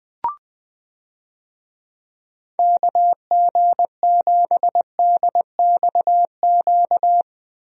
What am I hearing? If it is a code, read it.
Morse code, heard as a keyed tone, about 20 wpm, 700 Hz. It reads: KG7DXQ
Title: USA Call Signs - 6 Characters - Repeated 2x - 20wpm